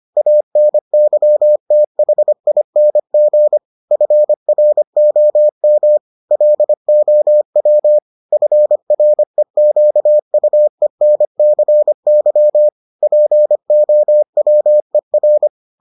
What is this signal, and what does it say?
Morse code, heard as a keyed tone, about 25 wpm, 600 Hz. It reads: ANYTHING FROM LOW FREQUENCY POWER